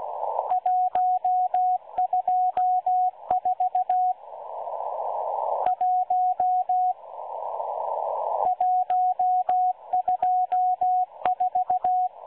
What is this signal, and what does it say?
Morse code, heard as a keyed tone, about 16 wpm, 730 Hz, with radio noise. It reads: 124 1 124
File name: M12-Morse-Numbers-Station-8116-kHz-CW